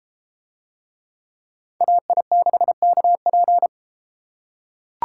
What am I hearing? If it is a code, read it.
Morse code, heard as a keyed tone, about 31 wpm, 700 Hz. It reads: AI6KP